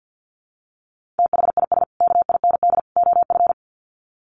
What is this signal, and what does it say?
Morse code, heard as a keyed tone, about 50 wpm, 700 Hz. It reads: THIS KIND OF
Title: Sets of 3 Words 50wpm